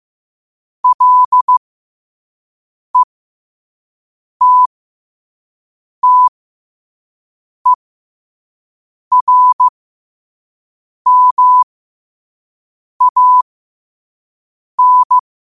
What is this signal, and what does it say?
Morse code, heard as a keyed tone, about 15 wpm, spaced as for 5 wpm, 1000 Hz. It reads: LETTERMAN
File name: qotdfiles / 5wpm